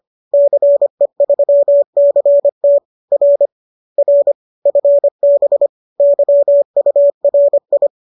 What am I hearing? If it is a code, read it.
Morse code, heard as a keyed tone, about 25 wpm, 570 Hz. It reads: CE3CT R R FB YURI